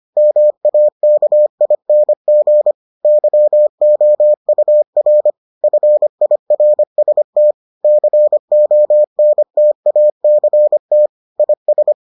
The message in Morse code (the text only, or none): MAKING YOUR FIRST CONTACT IS